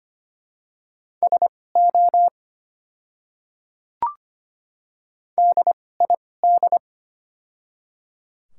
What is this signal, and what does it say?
Morse code, heard as a keyed tone, about 25 wpm, spaced as for 18 wpm, 700 Hz. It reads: SO DID
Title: 2x Character Spacing - Top 300 Words - 25wpm